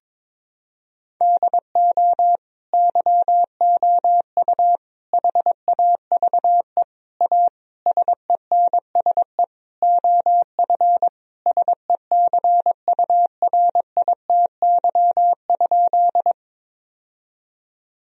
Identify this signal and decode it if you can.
Morse code, heard as a keyed tone, about 22 wpm, 700 Hz. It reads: DO YOU HAVE A SENSE OF SECURITY?
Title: Sentences from Top 1000 Words - Review Part 4 - 22wpm